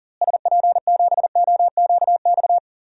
Morse code, heard as a keyed tone, about 40 wpm, 700 Hz. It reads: SP7OQX
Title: DXCC-100 Call Signs - 6 Characters - Repeated 2x - 40wpm